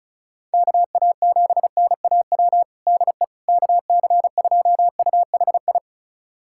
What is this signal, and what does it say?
Morse code, heard as a keyed tone, about 36 wpm, 700 Hz. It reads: KA7NAW DE KC2UHI
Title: USA Callsign DE Callsign 35wpm